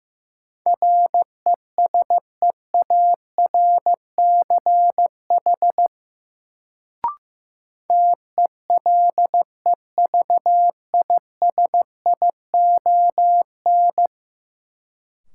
Morse code, heard as a keyed tone, about 15 wpm, 700 Hz. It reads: RESEARCH TELEVISION
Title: Sentences from Top 1000 Words - Review Part 4 - New Word Drill - 15wpm